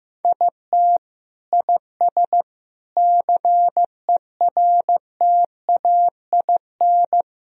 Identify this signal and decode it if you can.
Morse code, heard as a keyed tone, about 15 wpm, 700 Hz. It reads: IT IS CERTAIN
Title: Sentences from Top 1000 Words - Review Part 2 - 15wpm